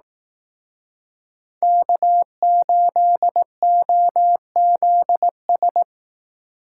Morse code, heard as a keyed tone, about 18 wpm, 700 Hz. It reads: K8OZS
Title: Speed Racing - USA Call Signs 15wpm